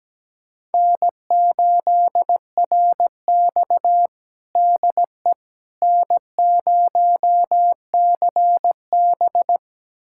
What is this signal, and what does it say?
Morse code, heard as a keyed tone, about 17 wpm, 700 Hz. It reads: N8RX DE N0CB
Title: USA Callsign DE Callsign - 4 Characters - 17wpm